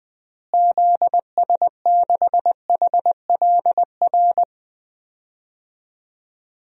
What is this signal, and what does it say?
Morse code, heard as a keyed tone, about 21 wpm, 700 Hz. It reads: ZS6HLR
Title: DXCC-100 Call Signs - 6 Characters - Repeated 3x - 20wpm